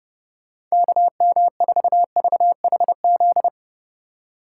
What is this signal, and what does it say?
Morse code, heard as a keyed tone, about 36 wpm, 700 Hz. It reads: KM4VHZ
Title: Speed Racing - USA 2x3 Call Signs 30wpm